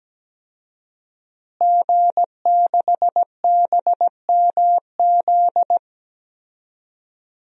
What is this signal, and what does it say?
Morse code, heard as a keyed tone, about 17 wpm, 700 Hz. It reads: G6BMZ
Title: DXCC-100 Call Signs - 5 Characters - Repeated 3x - 17wpm